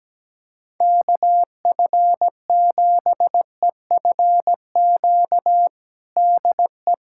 Morse code, heard as a keyed tone, about 17 wpm, 700 Hz. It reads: KF7EFQ DE
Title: USA Callsign DE Callsign 17wpm